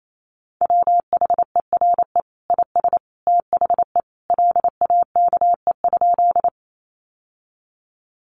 Morse code, heard as a keyed tone, about 28 wpm, 700 Hz. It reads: WHERE IS THE LAKE?
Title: Sentences from Top 1100 Words 28wpm